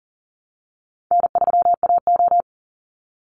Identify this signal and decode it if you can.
Morse code, heard as a keyed tone, about 40 wpm, 700 Hz. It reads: N3AO
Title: Easy Call Signs - 40wpm